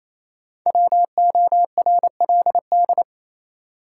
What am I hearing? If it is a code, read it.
Morse code, heard as a keyed tone, about 28 wpm, 700 Hz. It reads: WORLD
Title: Sets of 2 Words - New Word Drill - 28wpm